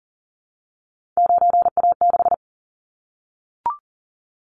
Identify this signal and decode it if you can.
Morse code, heard as a keyed tone, about 40 wpm, 700 Hz. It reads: 9A6
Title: Callsign Prefixes for Top 100 DXCC - 40wpm